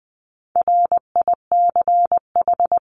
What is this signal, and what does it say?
Morse code, heard as a keyed tone, about 20 wpm, 700 Hz. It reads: RICH
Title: 260 Lesson 42e - Words 20-20wpm